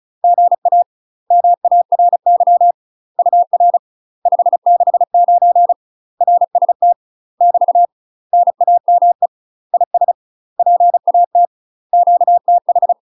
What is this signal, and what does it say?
Morse code, heard as a keyed tone, about 36 wpm, 700 Hz. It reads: GA MARY UR 569 RST <BT> NAME IS PAT QTH
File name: UK MASTER QSOs TRAINING / 35WPM / UK MASTER QSOs 0029